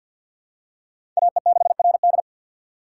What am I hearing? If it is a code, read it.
Morse code, heard as a keyed tone, about 47 wpm, 700 Hz. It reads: AE6RD
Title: USA Call Signs - 5 Characters - 3x Character Spacing - 50wpm